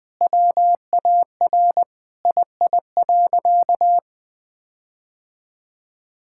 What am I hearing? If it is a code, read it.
Morse code, heard as a keyed tone, about 20 wpm, 700 Hz. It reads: WAR II.
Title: Sentences with 7-Word Phrases - 20wpm